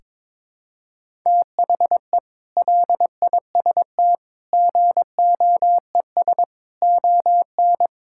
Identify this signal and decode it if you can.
Morse code, heard as a keyed tone, about 22 wpm, 700 Hz. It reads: THE LIST GOES ON
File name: practice / 22wpm / Sentences with 8-Word Phrases - 22wpm